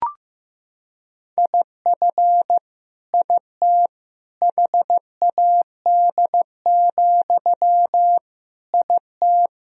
Morse code, heard as a keyed tone, about 15 wpm, 700 Hz. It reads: IF IT HAD, IT
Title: Sentences from Top 1000 Words - Review Part 3 - 15wpm